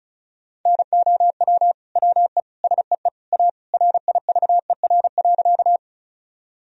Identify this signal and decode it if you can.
Morse code, heard as a keyed tone, about 36 wpm, 700 Hz. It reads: NOW WE SEE A RIVER.
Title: Sentences from Top 1000 Words - Review Part 1 - 35wpm